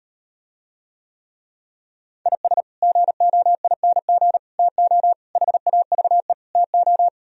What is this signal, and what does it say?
Morse code, heard as a keyed tone, about 38 wpm, 700 Hz. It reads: IS GOING TO HAVE TO